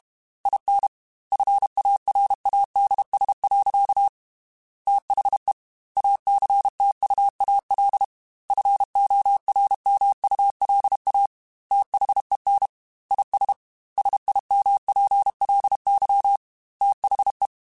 Morse code, heard as a keyed tone, about 32 wpm, 800 Hz. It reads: IN FARADS. THE ACTUAL FORMULA THEN IS SIMPLY THE
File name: Technical Subjects / Technical Subjects-mp3s / XC @30